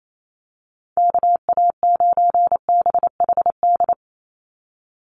Morse code, heard as a keyed tone, about 28 wpm, 700 Hz. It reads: KA9BHD
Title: Call Signs - 28wpm